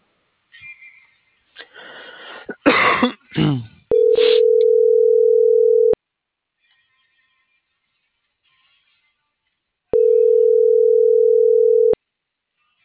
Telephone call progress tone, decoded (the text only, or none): ringback tone